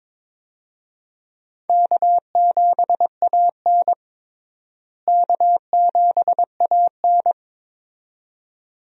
Morse code, heard as a keyed tone, about 22 wpm, 700 Hz. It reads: K7AN K7AN
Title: USA Call Signs - 4 Characters - Repeated 2x - 22wpm